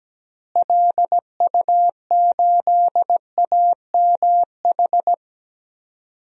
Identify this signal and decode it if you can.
Morse code, heard as a keyed tone, about 17 wpm, 700 Hz. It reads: LU8AMH
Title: DXCC-100 Call Signs - 6 Characters - Repeated 2x - 17wpm